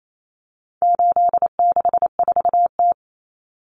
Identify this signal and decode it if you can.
Morse code, heard as a keyed tone, about 28 wpm, 700 Hz. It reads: 864T
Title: Common Cut Numbers - 4-Digit 28wpm